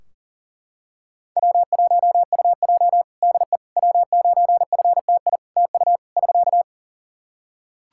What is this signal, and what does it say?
Morse code, heard as a keyed tone, about 43 wpm, 700 Hz. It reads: W1UJ DE W9FTI TU <SK>
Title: Practice QSO - Exchange 4 - Ending - 40wpm